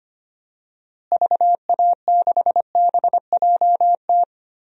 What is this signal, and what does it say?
Morse code, heard as a keyed tone, about 25 wpm, 700 Hz. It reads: VA6BJT
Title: DXCC-100 Call Signs - 6 Characters - Repeated 2x - 25wpm